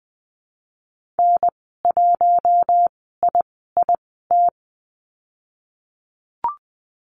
Morse code, heard as a keyed tone, about 19 wpm, spaced as for 14 wpm, 700 Hz. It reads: N1IIT
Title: 048 Lesson 9g - Number 4 - Call Signs 20wpm